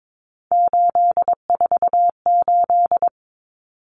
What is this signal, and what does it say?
Morse code, heard as a keyed tone, about 22 wpm, 700 Hz. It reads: 848
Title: Common Cut Numbers - 3-Digit 22wpm